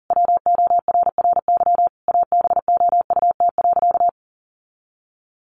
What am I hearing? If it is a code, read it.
Morse code, heard as a keyed tone, about 40 wpm, 700 Hz. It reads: WORRY ABOUT.
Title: Sentences from Top 1100 Words 40wpm